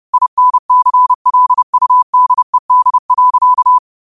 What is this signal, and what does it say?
Morse code, heard as a keyed tone, about 30 wpm, 1000 Hz. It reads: INCLUDED.
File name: qotdfiles / 30wpm